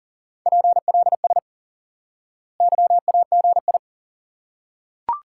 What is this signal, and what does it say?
Morse code, heard as a keyed tone, about 37 wpm, 700 Hz. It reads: PLS YAGI
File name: IQE Course 3 - Lesson 89 - YAGI - Two Elements - 40wpm